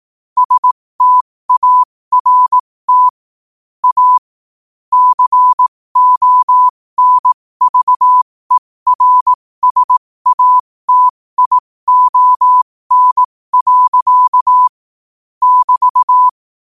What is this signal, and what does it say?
Morse code, heard as a keyed tone, about 18 wpm, spaced as for 15 wpm, 1000 Hz. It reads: START A CONVERSATION. <BT>
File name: qotdfiles / 15wpm